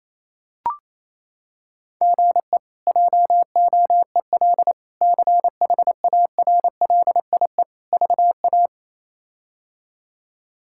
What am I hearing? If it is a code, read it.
Morse code, heard as a keyed tone, about 28 wpm, 700 Hz. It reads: GE JOEL CHARLIE VA
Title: K1USN SST QSO Practice - 28wpm